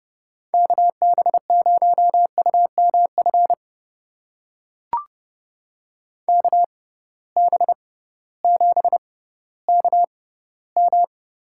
Morse code, heard as a keyed tone, about 31 wpm, spaced as for 16 wpm, 700 Hz. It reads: KB0UMF KB7KM
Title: USA Call Signs - 6 Characters - 6x Character Spacing - 30wpm